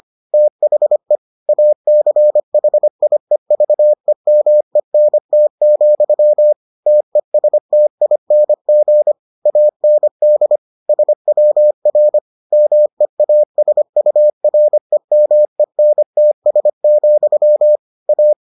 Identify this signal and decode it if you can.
Morse code, heard as a keyed tone, about 25 wpm, 600 Hz. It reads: THE ACHIEVEMENT, TESTING AND SWR MEASUREMENTS, A